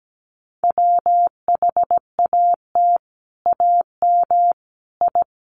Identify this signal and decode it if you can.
Morse code, heard as a keyed tone, about 17 wpm, 700 Hz. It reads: WHAT AM I
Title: Sets of 5 Words Encore2 17wpm